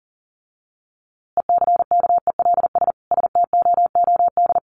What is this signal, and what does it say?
Morse code, heard as a keyed tone, about 40 wpm, 700 Hz. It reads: ECKELS STOOD
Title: A Sound Of Thunder 40wpm